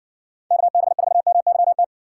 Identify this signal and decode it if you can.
Morse code, heard as a keyed tone, about 60 wpm, 700 Hz. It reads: KB4MYT